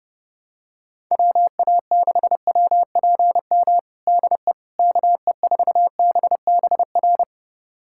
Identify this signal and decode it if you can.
Morse code, heard as a keyed tone, about 30 wpm, 700 Hz. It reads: WA6WPM DE KE4BBR
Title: USA Callsign DE Callsign - 6 Characters - 30wpm